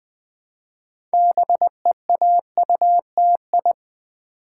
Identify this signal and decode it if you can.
Morse code, heard as a keyed tone, about 20 wpm, 700 Hz. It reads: BEAUTI
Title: Build A Word - 3 Letters At A Time - 9-Letter Words - 20wpm